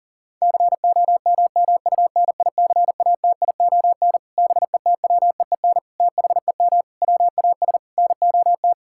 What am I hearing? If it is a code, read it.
Morse code, heard as a keyed tone, about 40 wpm, 700 Hz. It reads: COMMUNICATION BETWEEN THEM WAS NOT
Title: Sentences from Top 1000 Words - Review Part 6 - 40wpm